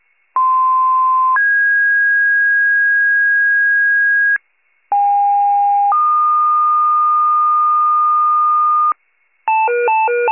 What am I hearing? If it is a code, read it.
Two-tone page, heard as a one-second tone, then a three-second tone, about 1040 then 1650 Hz, about 810 then 1220 Hz, with noise. It is heard from 0.4 to 8.9 s.
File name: pager / 95 RESCUE DUTY / wcalert